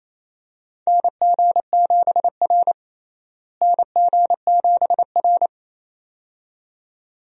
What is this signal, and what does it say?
Morse code, heard as a keyed tone, about 28 wpm, 700 Hz. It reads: NG7R NG7R
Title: USA Call Signs - 4 Characters - Repeated 2x - 28wpm